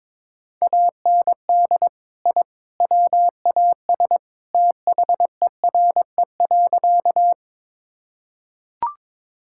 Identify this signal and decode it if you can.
Morse code, heard as a keyed tone, about 22 wpm, 700 Hz. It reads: AND I WAS THERE.
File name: practice / 22wpm / Sentences with 3-Word Phrases - 22wpm